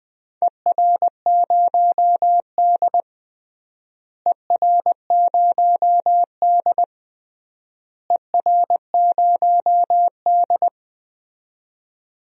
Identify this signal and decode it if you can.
Morse code, heard as a keyed tone, about 20 wpm, 700 Hz. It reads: ER0D ER0D ER0D
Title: DXCC-100 Call Signs - 4 Characters - Repeated 3x - 20wpm